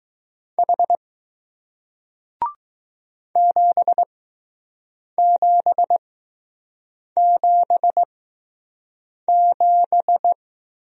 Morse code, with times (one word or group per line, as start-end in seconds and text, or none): 0.6-1.0 H
3.4-10.3 7777